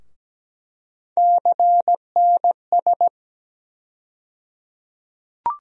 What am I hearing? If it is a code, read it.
Morse code, heard as a keyed tone, about 17 wpm, 700 Hz. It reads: CNS
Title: USA Call Sign Postfixes - 3 Letters 17wpm